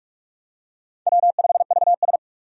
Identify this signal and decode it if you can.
Morse code, heard as a keyed tone, about 46 wpm, 700 Hz. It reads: W5VS
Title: USA Call Signs - 4 Characters - 45wpm